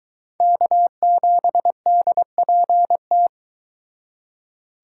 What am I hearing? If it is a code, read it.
Morse code, heard as a keyed tone, about 22 wpm, 700 Hz. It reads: K7DPT